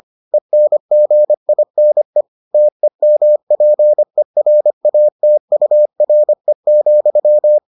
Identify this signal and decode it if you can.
Morse code, heard as a keyed tone, about 25 wpm, 600 Hz. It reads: ENGINE TEMPERATURE,